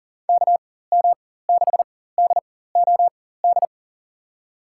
Morse code, heard as a keyed tone, about 40 wpm, spaced as for 19 wpm, 690 Hz. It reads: KM6DOD
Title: USA Call Signs - 4x Character Spacing - 40wpm